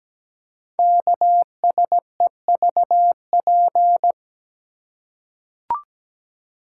Morse code, heard as a keyed tone, about 17 wpm, 700 Hz. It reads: KSEVP
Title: Code Groups - 5 Letter:Numbers - 17wpm